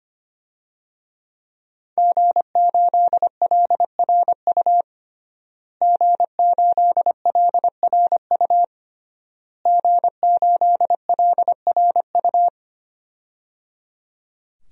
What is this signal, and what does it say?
Morse code, heard as a keyed tone, about 25 wpm, 700 Hz. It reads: G8LRU G8LRU G8LRU
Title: DXCC-100 Call Signs - 5 Characters - Repeated 3x - 25wpm